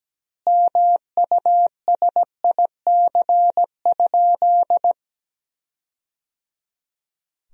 Morse code, heard as a keyed tone, about 17 wpm, 700 Hz. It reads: MUSIC?
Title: 2x Word-Spacing - Sentences from Top 300 Words - 17wpm